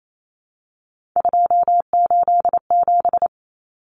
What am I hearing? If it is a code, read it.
Morse code, heard as a keyed tone, about 28 wpm, 700 Hz. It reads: 287
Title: Numbers - 3-Digit 28wpm